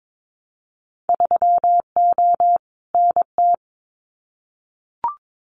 Morse code, heard as a keyed tone, about 22 wpm, 700 Hz. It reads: 3O NT
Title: Winter Field Day Exchanges Encore 22wpm